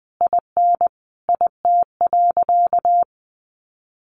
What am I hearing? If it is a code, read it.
Morse code, heard as a keyed tone, about 20 wpm, 700 Hz. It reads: IN IT.
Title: Sentences from Top 1400 Words 20wpm